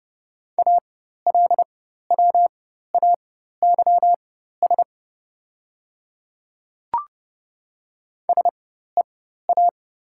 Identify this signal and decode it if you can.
Morse code, heard as a keyed tone, about 30 wpm, spaced as for 14 wpm, 700 Hz. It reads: ALWAYS SEA
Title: 4x Character Spacing - Top 300 Words - 30wpm